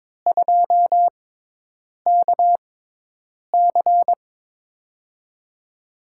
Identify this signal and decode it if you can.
Morse code, heard as a keyed tone, about 23 wpm, spaced as for 8 wpm, 700 Hz. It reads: 2KC
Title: USA Call Signs - 5 Characters - 6x Character Spacing - 22wpm